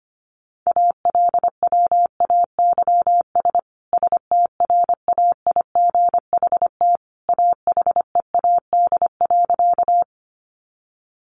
Morse code, heard as a keyed tone, about 25 wpm, 700 Hz. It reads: ALWAYS STRAIGHT AHEAD.
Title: Sentences from Top 1100 Words 25wpm